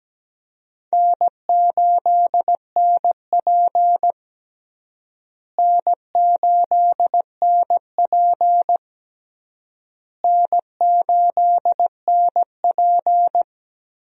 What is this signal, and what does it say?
Morse code, heard as a keyed tone, about 17 wpm, 700 Hz. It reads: N8NP N8NP N8NP
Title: USA Call Signs - 4 Characters - Repeated 3x - 17wpm